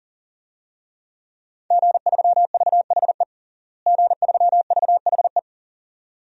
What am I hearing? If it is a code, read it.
Morse code, heard as a keyed tone, about 41 wpm, 700 Hz. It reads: G3VHE G3VHE
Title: DXCC-100 Call Signs - 5 Characters - Repeated 2x - 40wpm